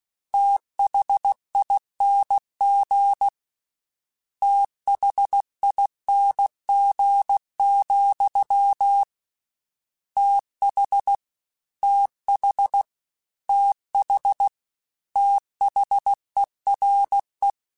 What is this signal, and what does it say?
Morse code, heard as a keyed tone, about 16 wpm, 800 Hz. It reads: THING THING, TH TH TH THERE
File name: Combinations / Combinations-mp3s / Combinations@15